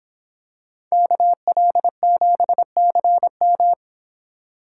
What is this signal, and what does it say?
Morse code, heard as a keyed tone, about 25 wpm, 700 Hz. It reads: KL7CM